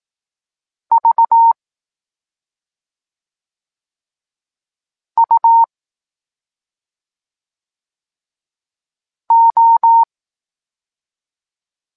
Morse code, heard as a keyed tone, about 18 wpm, 920 Hz. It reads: V U O
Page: Tekst 14 – doskonalenie odbioru poznanych liter